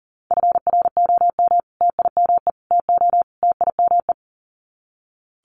Morse code, heard as a keyed tone, about 40 wpm, 700 Hz. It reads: FROM TIME TO TIME
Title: Sets of 4 Words 40wpm